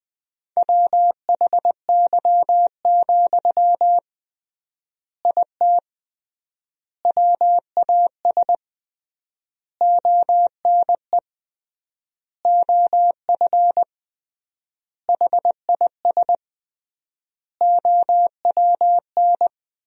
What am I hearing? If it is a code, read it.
Morse code, heard as a keyed tone, about 20 wpm, 700 Hz. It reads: WHY, IT WAS ONE OF HIS OWN
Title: 3x Word-Spacing - Sentences from Top 300 Words - 20wpm